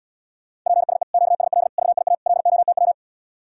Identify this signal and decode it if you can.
Morse code, heard as a keyed tone, about 80 wpm, 700 Hz. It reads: PSE QSL VIA LOTW